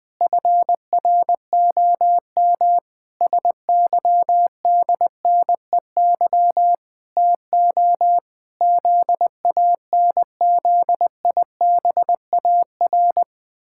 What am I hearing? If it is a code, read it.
Morse code, heard as a keyed tone, about 20 wpm, 700 Hz. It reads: FROM SYDNEY TO ZANZIBAR
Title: Pangram Challenge v2 - 20wpm